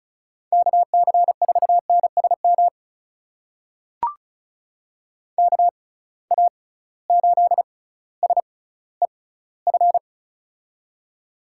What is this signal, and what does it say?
Morse code, heard as a keyed tone, about 36 wpm, spaced as for 18 wpm, 700 Hz. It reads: KC4NSM KA8SEF
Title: USA Call Signs - 6 Characters - 6x Character Spacing - 35wpm